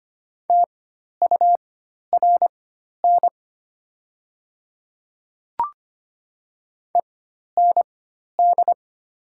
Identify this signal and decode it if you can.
Morse code, heard as a keyed tone, about 25 wpm, spaced as for 12 wpm, 700 Hz. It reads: TURN END
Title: 4x Character Spacing - Top 200 Words - 25wpm